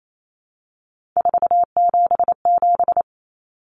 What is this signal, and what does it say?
Morse code, heard as a keyed tone, about 28 wpm, 700 Hz. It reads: 477
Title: Common Cut Numbers - 3-Digit 28wpm